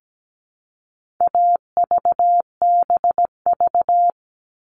Morse code, heard as a keyed tone, about 17 wpm, 700 Hz. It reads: AVBV
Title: All Cut Numbers - 4-Digit 17wpm